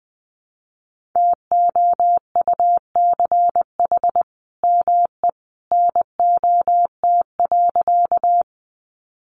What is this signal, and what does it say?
Morse code, heard as a keyed tone, about 20 wpm, 700 Hz. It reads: TOUCH ME NOT.
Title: Sentences from Top 1200 Words 20wpm